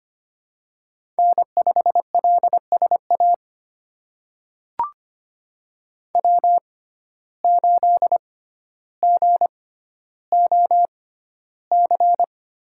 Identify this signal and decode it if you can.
Morse code, heard as a keyed tone, about 25 wpm, spaced as for 13 wpm, 700 Hz. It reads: N5LSA W8GOC
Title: USA Call Signs - 6x Character Spacing - 25wpm